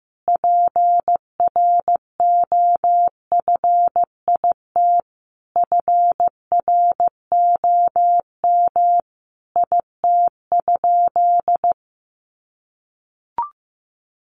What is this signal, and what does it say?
Morse code, heard as a keyed tone, about 15 wpm, 700 Hz. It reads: PROFIT FROM IT?
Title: Sentences from Top 800 Words 15wpm